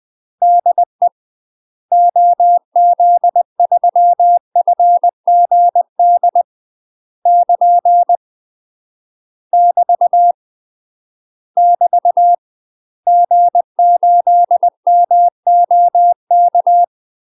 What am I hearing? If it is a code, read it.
Morse code, heard as a keyed tone, about 20 wpm, 700 Hz. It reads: DE OZ3FGD <KN> <BT> <BT> G8MOK
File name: UK MASTER QSOs TRAINING / 20WPM / UK MASTER QSOs 0057